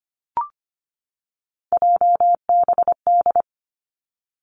Morse code, heard as a keyed tone, about 25 wpm, 700 Hz. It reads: JBD
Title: Three Challenging Letter-Number Combinations 25wpm